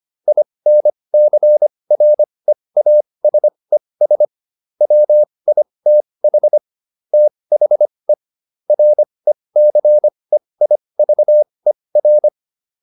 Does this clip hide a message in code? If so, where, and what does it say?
Yes: Morse code from 0.3 to 12.3 s, heard as a keyed tone, about 25 wpm, spaced as for 20 wpm, 600 Hz. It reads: INCREASES WITH THE RECEIVER